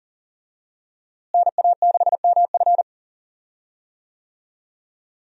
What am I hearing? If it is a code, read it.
Morse code, heard as a keyed tone, about 40 wpm, 700 Hz. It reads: NA6MF
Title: USA Call Signs - 5 Characters - Repeated 2x - 40wpm